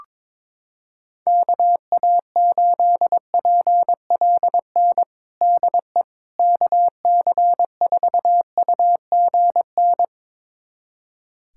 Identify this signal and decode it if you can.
Morse code, heard as a keyed tone, about 21 wpm, 700 Hz. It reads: KA8PLN DE KC4UGN
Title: USA Callsign DE Callsign 22wpm